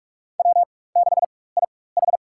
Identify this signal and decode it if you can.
Morse code, heard as a keyed tone, about 46 wpm, spaced as for 21 wpm, 700 Hz. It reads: W6IH
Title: USA Call Signs - 4 Characters - 4x Character Spacing - 45wpm